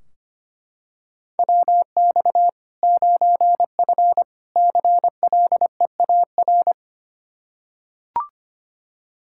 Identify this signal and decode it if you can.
Morse code, heard as a keyed tone, about 25 wpm, 700 Hz. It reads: WX 9F CLEAR
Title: K1USN SST QSO Practice 25wpm